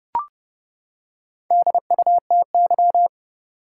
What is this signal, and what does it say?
Morse code, heard as a keyed tone, about 30 wpm, 700 Hz. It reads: DUTY
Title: Four Letter Words - 30wpm